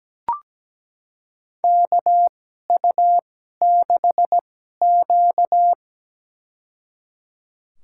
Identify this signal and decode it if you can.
Morse code, heard as a keyed tone, about 17 wpm, spaced as for 12 wpm, 700 Hz. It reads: KU6Q
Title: USA Call Signs - 4 Characters - 2x Character Spacing - 17wpm